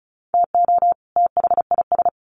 Morse code, heard as a keyed tone, about 34 wpm, 700 Hz. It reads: TO THIS
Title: Sets of 2 Words 35wpm